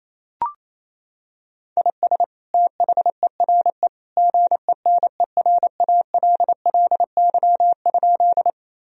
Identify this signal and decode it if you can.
Morse code, heard as a keyed tone, about 28 wpm, 700 Hz. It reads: IS THERE GENERALLY?
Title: Sentences from Top 1000 Words - Review Part 4 - 28wpm